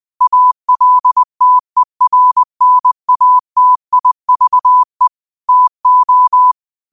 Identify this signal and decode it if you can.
Morse code, heard as a keyed tone, about 20 wpm, 1000 Hz. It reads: ALTERNATIVE TO
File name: qotdfiles / 20wpm